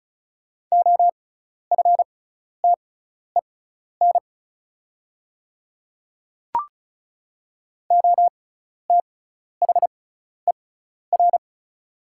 Morse code, heard as a keyed tone, about 36 wpm, spaced as for 12 wpm, 700 Hz. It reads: OFTEN OTHER
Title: 6x Character Spacing - Top 300 Words - 35wpm